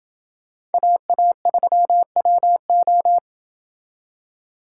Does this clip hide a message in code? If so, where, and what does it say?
Yes: Morse code from 0.7 to 3.2 s, heard as a keyed tone, about 29 wpm, 700 Hz. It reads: AA3WO